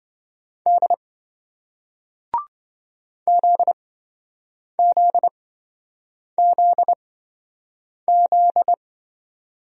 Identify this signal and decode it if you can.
Morse code, heard as a keyed tone, about 25 wpm, spaced as for 7 wpm, 700 Hz. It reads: D ZZZZ
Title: Speed-Racing Single Letters 20wpm